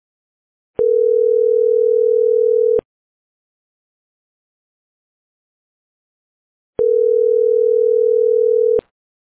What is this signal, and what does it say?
Ringback tone (North America), 440 + 480 Hz.